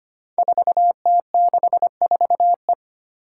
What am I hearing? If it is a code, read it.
Morse code, heard as a keyed tone, about 25 wpm, 700 Hz. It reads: 4T64E
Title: Five Challenging Letter-Number Combinations 25wpm